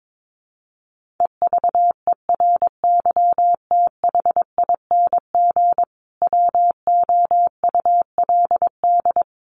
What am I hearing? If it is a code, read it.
Morse code, heard as a keyed tone, about 22 wpm, 700 Hz. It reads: EVERYTHING WOULD
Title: Sentences from Top 1400 Words 22wpm